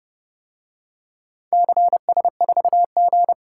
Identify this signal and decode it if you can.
Morse code, heard as a keyed tone, about 29 wpm, 700 Hz. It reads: CS4G